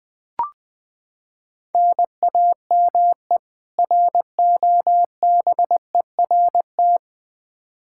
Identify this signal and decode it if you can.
Morse code, heard as a keyed tone, about 20 wpm, 700 Hz. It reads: NAME ROBERT
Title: Practice QSO - Exchange 1 - Beginning - 20wpm